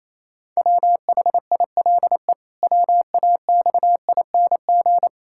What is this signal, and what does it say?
Morse code, heard as a keyed tone, about 28 wpm, 700 Hz. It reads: WHILE WAXING
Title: Pangram Challenge v2 - 28wpm